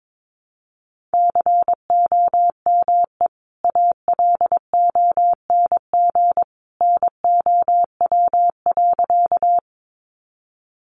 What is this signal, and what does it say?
Morse code, heard as a keyed tone, about 22 wpm, 700 Hz. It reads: COME ALONG NOW.
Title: Sentences from Top 700 Words 22wpm